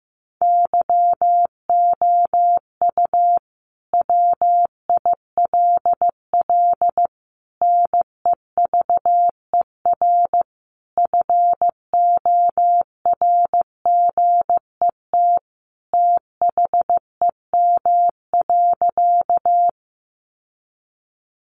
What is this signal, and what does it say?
Morse code, heard as a keyed tone, about 15 wpm, 700 Hz. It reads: YOU WILL NEVER FORGET THEM.
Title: Sentences from Top 1200 Words 15wpm